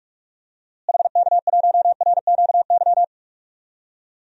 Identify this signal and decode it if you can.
Morse code, heard as a keyed tone, about 46 wpm, 700 Hz. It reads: HK1RQY
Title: DXCC-100 Call Signs - 6 Characters - Repeated 2x - 45wpm